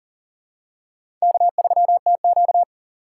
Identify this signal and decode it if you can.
Morse code, heard as a keyed tone, about 40 wpm, 700 Hz. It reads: K3TQ
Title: USA Call Signs - 4 Characters - 3x Character Spacing - 40wpm